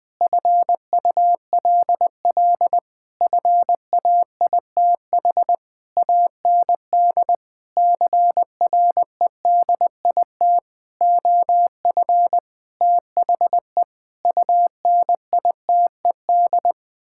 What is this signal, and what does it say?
Morse code, heard as a keyed tone, about 20 wpm, 700 Hz. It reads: FULL FAITH AND CREDIT OF THE UNITED
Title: Sentences with 9-Word Phrases - 20wpm